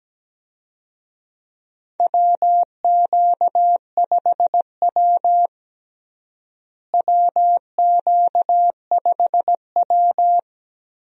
Morse code, heard as a keyed tone, about 17 wpm, 700 Hz. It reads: WQ5W WQ5W
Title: USA Call Signs - 4 Characters - Repeated 2x - 17wpm